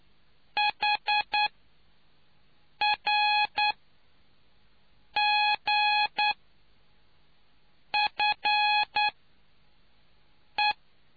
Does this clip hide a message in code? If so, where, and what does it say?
Yes: Morse code from 0.6 to 10.7 s, heard as a keyed tone, about 9 wpm, spaced as for 4 wpm, 840 Hz. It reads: HRGFE